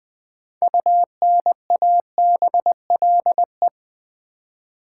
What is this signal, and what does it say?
Morse code, heard as a keyed tone, about 20 wpm, 700 Hz. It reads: UNABLE
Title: 6 Letter Words - First 3 Letters Given - 20wpm